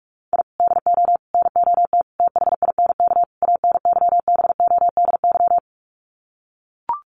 Morse code, heard as a keyed tone, about 44 wpm, 700 Hz. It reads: I DO NOT THINK ANYBODY
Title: Sets of 5 Words 45wpm